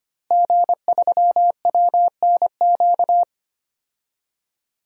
Morse code, heard as a keyed tone, about 25 wpm, 700 Hz. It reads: G3WNQ
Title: DXCC-100 Call Signs - 5 Characters - Repeated 3x - 25wpm